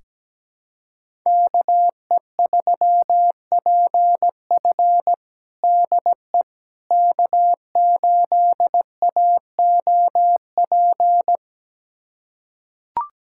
Morse code, heard as a keyed tone, about 17 wpm, 700 Hz. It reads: KE3PF DE K8AOP
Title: USA Callsign DE Callsign - 5 Characters - 17wpm